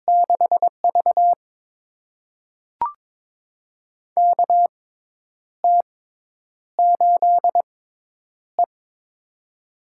Morse code, heard as a keyed tone, about 22 wpm, spaced as for 9 wpm, 700 Hz. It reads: 6V KT8E
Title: USA Call Signs - 4 Characters - 6x Character Spacing - 22wpm